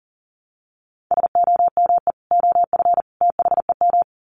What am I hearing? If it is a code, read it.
Morse code, heard as a keyed tone, about 40 wpm, 700 Hz. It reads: SOME OF THEM
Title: Sets of 3 Words 40wpm